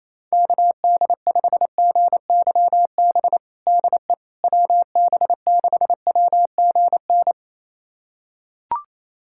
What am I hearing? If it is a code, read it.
Morse code, heard as a keyed tone, about 28 wpm, 700 Hz. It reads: KD5GYB DE WB6WGN
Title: USA Callsign DE Callsign - 6 Characters - 28wpm